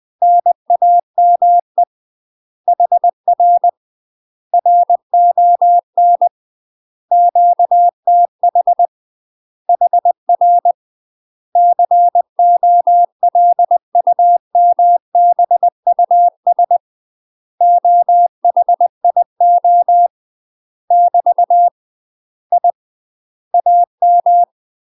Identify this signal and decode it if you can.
Morse code, heard as a keyed tone, about 20 wpm, 700 Hz. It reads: NAME HR RON QTH HR COLUMBUS OHIO <BT> I AM
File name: UK MASTER QSOs TRAINING / 20WPM / UK MASTER QSOs 0063